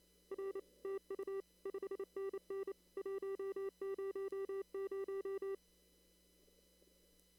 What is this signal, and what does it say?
Morse code, heard as a keyed tone, about 28 wpm, 390 Hz. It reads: R TU 5NN 100
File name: CW / kt5j wpx cw-op-k5pi